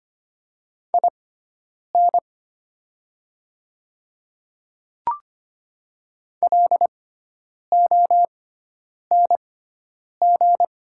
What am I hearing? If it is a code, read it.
Morse code, heard as a keyed tone, about 25 wpm, spaced as for 9 wpm, 700 Hz. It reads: IN LONG
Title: 6x Character Spacing - Top 100 Words - 25wpm